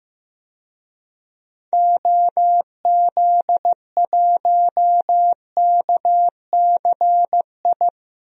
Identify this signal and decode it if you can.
Morse code, heard as a keyed tone, about 15 wpm, 690 Hz. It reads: OZ1KCI
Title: DXCC-100 Call Signs - 6 Characters - 15wpm